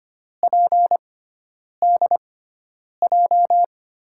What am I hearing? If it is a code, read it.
Morse code, heard as a keyed tone, about 25 wpm, spaced as for 9 wpm, 700 Hz. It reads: PDJ
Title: USA Call Signs - 5 Characters - 6x Character Spacing - 25wpm